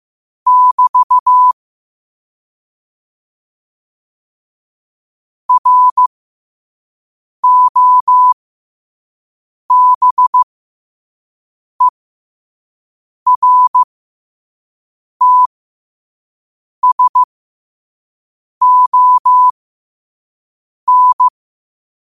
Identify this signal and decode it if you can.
Morse code, heard as a keyed tone, about 15 wpm, spaced as for 5 wpm, 1000 Hz. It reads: <BT> ROBERTSON
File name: qotdfiles / 5wpm